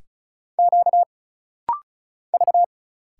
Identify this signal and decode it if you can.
Morse code, heard as a keyed tone, about 35 wpm, 700 Hz. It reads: Q V
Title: Single Letters - Rapid-Fire 35wpm